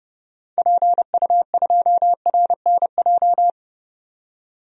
Morse code, heard as a keyed tone, about 30 wpm, 700 Hz. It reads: PU2RNJ
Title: DXCC-100 Call Signs - 6 Characters - 30wpm